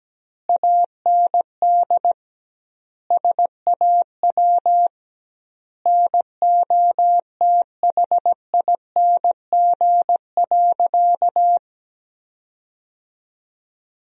Morse code, heard as a keyed tone, about 17 wpm, 700 Hz. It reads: AND SAW NOTHING.
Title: 2x Word-Spacing - Sentences from Top 500 Words - 17wpm